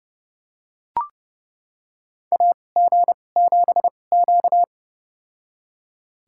Morse code, heard as a keyed tone, about 30 wpm, spaced as for 22 wpm, 700 Hz. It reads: AG7Q
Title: USA Call Signs - 4 Characters - 2x Character Spacing - 30wpm